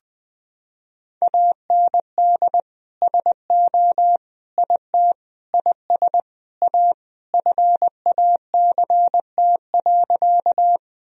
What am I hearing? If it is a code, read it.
Morse code, heard as a keyed tone, about 20 wpm, 700 Hz. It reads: AND SO IT IS A FACT.
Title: Sentences from Top 1000 Words - Review Part 3 - 20wpm